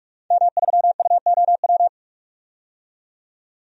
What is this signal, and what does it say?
Morse code, heard as a keyed tone, about 46 wpm, 700 Hz. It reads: M3UOW
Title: DXCC-100 Call Signs - 5 Characters - 45wpm